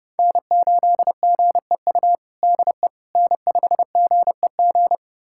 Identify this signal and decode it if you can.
Morse code, heard as a keyed tone, about 30 wpm, 700 Hz. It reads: N8GEU DE N5GEG
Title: USA Callsign DE Callsign - 5 Characters - 30wpm